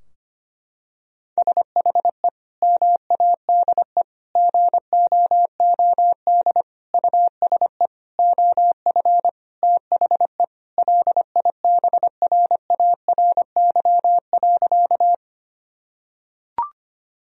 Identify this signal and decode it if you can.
Morse code, heard as a keyed tone, about 25 wpm, 700 Hz. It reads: SHE MADE GOOD USE OF THE LIBRARY.
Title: Sentences from Top 1000 Words - Review Part 4 - 25wpm